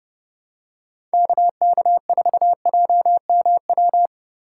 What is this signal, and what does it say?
Morse code, heard as a keyed tone, about 30 wpm, 700 Hz. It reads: KK4JMW
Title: USA Call Signs - 6 Characters - 2x Character Spacing - 30wpm